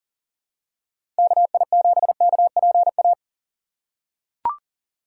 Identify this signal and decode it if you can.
Morse code, heard as a keyed tone, about 40 wpm, 700 Hz. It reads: KI7KPA
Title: USA Call Signs - 6 Characters - 3x Character Spacing - 40wpm